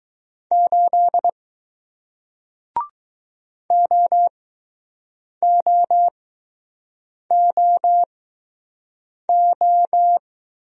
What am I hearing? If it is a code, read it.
Morse code, heard as a keyed tone, about 20 wpm, spaced as for 6 wpm, 700 Hz. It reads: 8 OOOO